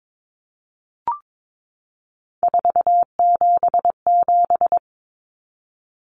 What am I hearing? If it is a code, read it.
Morse code, heard as a keyed tone, about 22 wpm, 700 Hz. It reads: 477
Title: Common Cut Numbers - 3-Digit 22wpm